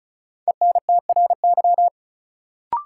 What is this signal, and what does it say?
Morse code, heard as a keyed tone, about 34 wpm, 700 Hz. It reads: ENTRY
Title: Sentences from Top 1500 Words - New Word Drill - 35wpm